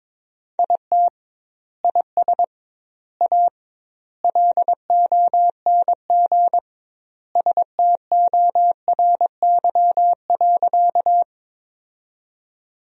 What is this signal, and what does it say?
Morse code, heard as a keyed tone, about 22 wpm, 700 Hz. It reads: IT IS A LONG STORY.
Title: 2x Word-Spacing - Sentences from Top 300 Words - 22wpm